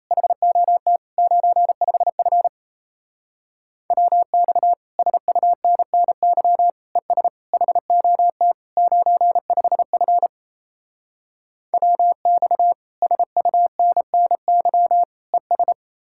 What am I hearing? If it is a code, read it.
Morse code, heard as a keyed tone, about 36 wpm, 700 Hz. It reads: HOT 95F WX SUNNY ES HOT 95F WX SUNNY ES